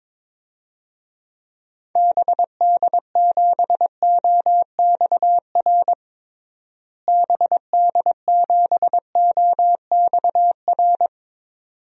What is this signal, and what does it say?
Morse code, heard as a keyed tone, about 22 wpm, 700 Hz. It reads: BD7OXR BD7OXR
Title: DXCC-100 Call Signs - 6 Characters - Repeated 2x - 22wpm